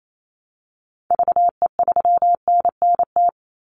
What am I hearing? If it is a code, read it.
Morse code, heard as a keyed tone, about 28 wpm, 700 Hz. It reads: VE3NNT
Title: Call Signs - 28wpm